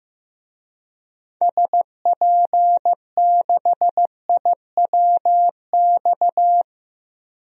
Morse code, heard as a keyed tone, about 15 wpm, 700 Hz. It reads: SP6IWX
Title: DXCC-100 Call Signs - 6 Characters - 15wpm